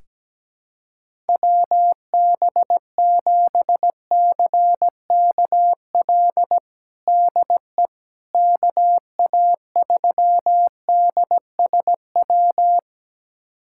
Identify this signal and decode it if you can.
Morse code, heard as a keyed tone, about 17 wpm, 700 Hz. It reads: WB7CKL DE KA3DSW
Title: USA Callsign DE Callsign - 6 Characters - 17wpm